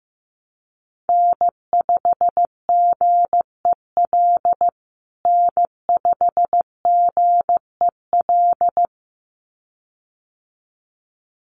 Morse code, heard as a keyed tone, about 15 wpm, 700 Hz. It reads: N5GEL N5GEL
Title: USA Call Signs Repeated 2x 15wpm